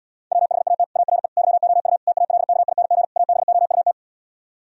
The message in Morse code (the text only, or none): WHAT ARE YOU TALKING ABOUT